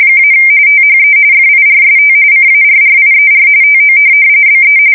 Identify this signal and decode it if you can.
RTTY, about 45.45 baud. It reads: - THE FIRST CARRIER TO BE
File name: fsk170 / USS-New-Jersey